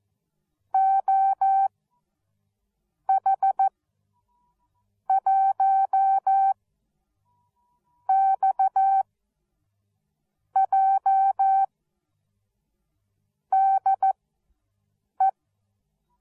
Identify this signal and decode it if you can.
Morse code, heard as a keyed tone, about 14 wpm, spaced as for 5 wpm, 780 Hz. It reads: OH1XJ DE